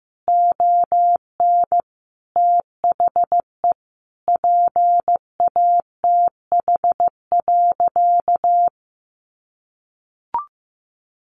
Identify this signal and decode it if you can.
Morse code, heard as a keyed tone, about 15 wpm, 700 Hz. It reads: ON THE PATH.
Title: Sentences from Top 1400 Words 15wpm